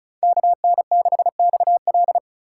Morse code, heard as a keyed tone, about 36 wpm, 700 Hz. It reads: KN6XL
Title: USA Call Signs - 4x Character Spacing - 35wpm